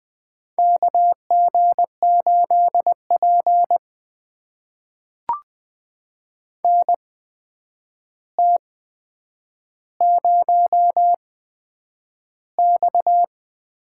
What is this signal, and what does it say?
Morse code, heard as a keyed tone, about 20 wpm, spaced as for 9 wpm, 700 Hz. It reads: KG8P NT0X
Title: USA Call Signs - 4 Characters - 8x Character Spacing - 20wpm